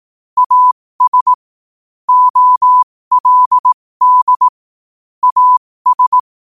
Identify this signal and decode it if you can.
Morse code, heard as a keyed tone, about 18 wpm, spaced as for 15 wpm, 1000 Hz. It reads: AS OLD AS
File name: qotdfiles / 15wpm